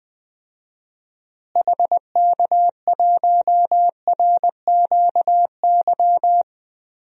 Morse code, heard as a keyed tone, about 21 wpm, 700 Hz. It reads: HK1RQY
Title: DXCC-100 Call Signs - 6 Characters - Repeated 2x - 20wpm